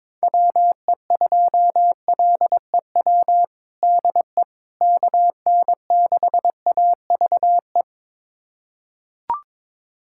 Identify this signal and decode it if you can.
Morse code, heard as a keyed tone, about 21 wpm, 700 Hz. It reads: WE2LEW DE KN6AVE
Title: USA Callsign DE Callsign - 6 Characters - 22wpm